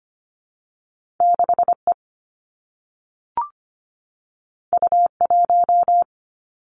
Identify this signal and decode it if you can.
Morse code, heard as a keyed tone, about 25 wpm, 700 Hz. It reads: 6 E U 1
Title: Two Challenging Letter-Number Combinations 25wpm